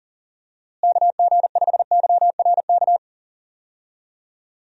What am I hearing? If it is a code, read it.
Morse code, heard as a keyed tone, about 40 wpm, 700 Hz. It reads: KG5YRK
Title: USA Call Signs - 6 Characters - Repeated 3x - 40wpm